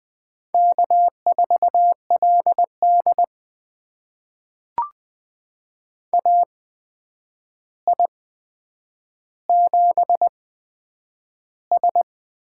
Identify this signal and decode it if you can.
Morse code, heard as a keyed tone, about 20 wpm, spaced as for 9 wpm, 700 Hz. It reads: K4LD AI7S
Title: USA Call Signs - 4 Characters - 8x Character Spacing - 20wpm